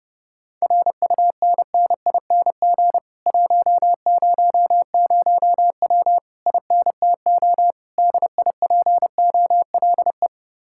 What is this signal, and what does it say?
Morse code, heard as a keyed tone, about 30 wpm, 700 Hz. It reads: RUNNING 100W INTO DIPOLE